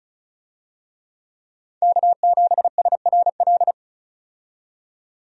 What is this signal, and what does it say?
Morse code, heard as a keyed tone, about 36 wpm, 700 Hz. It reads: K7SRL
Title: USA Call Signs - 5 Characters - 35wpm